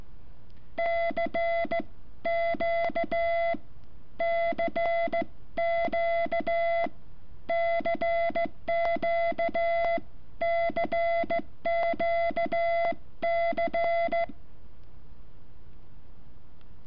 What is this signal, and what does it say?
Morse code, heard as a keyed tone, about 15 wpm, spaced as for 11 wpm, 680 Hz. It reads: CQCQCQCQC
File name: curso / ejercicio 14